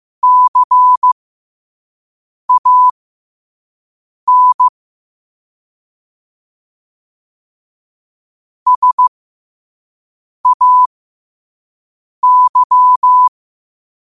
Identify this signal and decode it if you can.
Morse code, heard as a keyed tone, about 15 wpm, spaced as for 5 wpm, 1000 Hz. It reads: CAN SAY
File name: qotdfiles / 5wpm